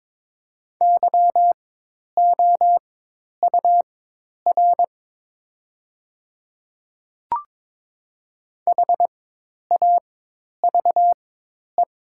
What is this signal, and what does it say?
Morse code, heard as a keyed tone, about 22 wpm, spaced as for 10 wpm, 700 Hz. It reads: YOUR HAVE
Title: 4x Character Spacing - Top 100 Words - 22wpm